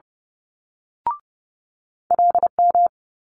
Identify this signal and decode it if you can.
Morse code, heard as a keyed tone, about 30 wpm, 700 Hz. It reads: LM
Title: Multi-letter Phonemes Quick Fire 30wpm